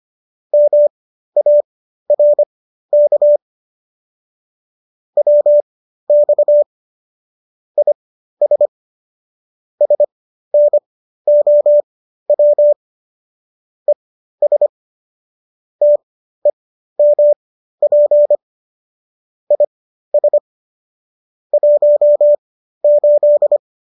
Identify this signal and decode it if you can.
Morse code, heard as a keyed tone, about 25 wpm, spaced as for 13 wpm, 590 Hz. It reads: MARK WX IS SNOW ES TEMP IS 18